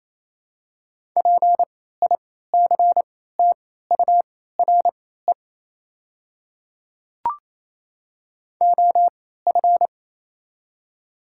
Morse code, heard as a keyed tone, about 27 wpm, spaced as for 16 wpm, 690 Hz. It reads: PICTURE OF
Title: 3x Character Spacing - Top 200 Words - 28wpm